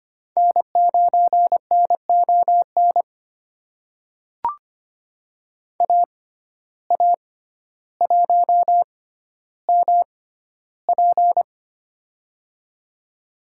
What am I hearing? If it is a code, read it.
Morse code, heard as a keyed tone, about 25 wpm, spaced as for 13 wpm, 700 Hz. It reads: N9NON AA1MP
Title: USA Call Signs - 5 Characters - 6x Character Spacing - 25wpm